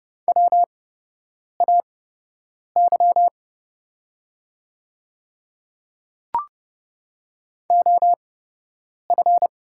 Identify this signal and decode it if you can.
Morse code, heard as a keyed tone, about 30 wpm, spaced as for 8 wpm, 700 Hz. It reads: WAY OF
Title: 8x Character Spacing - Top 100 Words - 30wpm